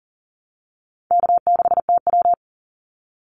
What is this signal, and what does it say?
Morse code, heard as a keyed tone, about 42 wpm, 700 Hz. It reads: K6TW
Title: NA SOTA Activator Callsigns 40wpm